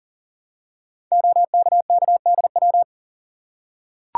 Morse code, heard as a keyed tone, about 39 wpm, 700 Hz. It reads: OKKDW
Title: Code Groups - 5 Letter:Numbers - 40wpm